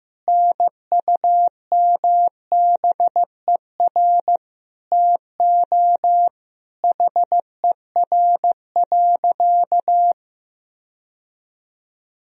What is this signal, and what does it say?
Morse code, heard as a keyed tone, about 15 wpm, 700 Hz. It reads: NUMBER TO HER.
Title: Sentences from Top 1000 Words - Review Part 1 - 15wpm